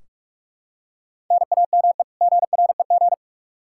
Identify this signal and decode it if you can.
Morse code, heard as a keyed tone, about 43 wpm, 700 Hz. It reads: NAME GREG
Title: Practice QSO - Exchange 1 - Beginning - 45wpm